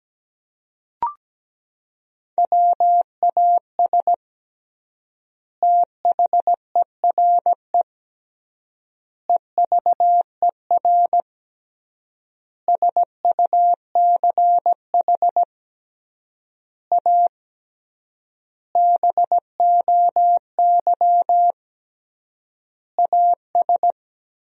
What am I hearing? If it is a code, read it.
Morse code, heard as a keyed tone, about 17 wpm, 700 Hz. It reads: WAS THERE EVER SUCH A BOY AS
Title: 3x Word-Spacing - Sentences from Top 400 Words - 17wpm